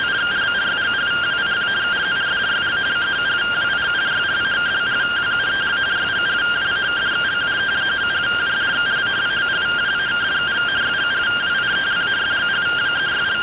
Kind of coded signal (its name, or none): two-tone teleprinter signal